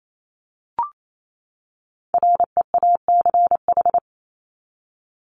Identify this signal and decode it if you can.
Morse code, heard as a keyed tone, about 28 wpm, 700 Hz. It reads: REACH
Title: ED Suffix - Top 100 Words with Base Words 28wpm